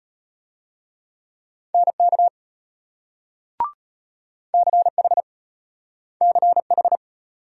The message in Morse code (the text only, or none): NK CH CH